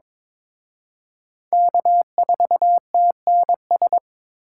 Morse code, heard as a keyed tone, about 26 wpm, 700 Hz. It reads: K4TNS
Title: Speed Racing - USA 1x3 Call Signs 22wpm